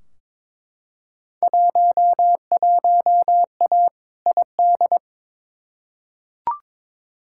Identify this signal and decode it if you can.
Morse code, heard as a keyed tone, about 21 wpm, 700 Hz. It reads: 11A ID
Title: Field Day Exchanges Encore - 22wpm